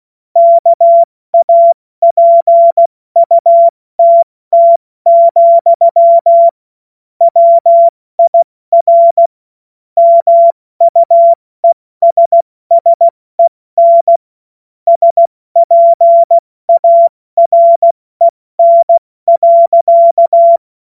Morse code, heard as a keyed tone, about 16 wpm, 680 Hz. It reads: KAPUTT, WIR MUESSEN SPAREN.
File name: Witz-08-Juli-2025-70-F80